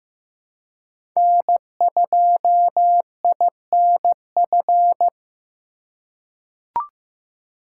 Morse code, heard as a keyed tone, about 15 wpm, 700 Hz. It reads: N2INF
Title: USA Call Signs - 3x Character Spacing - 15wpm